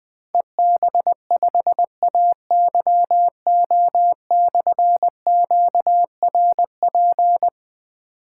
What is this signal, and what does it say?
Morse code, heard as a keyed tone, about 20 wpm, 700 Hz. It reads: EB5AYO/QRP
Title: Difficult Call Signs 20wpm